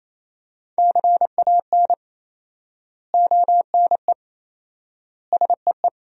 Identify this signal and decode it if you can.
Morse code, heard as a keyed tone, about 27 wpm, 700 Hz. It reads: CAN ONE SEE
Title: 4x Word-Spacing - Sentences from Top 100 Words - 28wpm